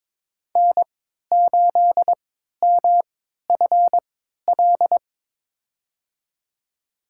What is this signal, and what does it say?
Morse code, heard as a keyed tone, about 22 wpm, spaced as for 13 wpm, 700 Hz. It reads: N8MFL
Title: USA Call Signs - 3x Character Spacing - 22wpm